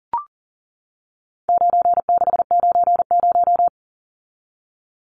Morse code, heard as a keyed tone, about 40 wpm, 700 Hz. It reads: 9690
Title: Numbers - 4-Digit 40wpm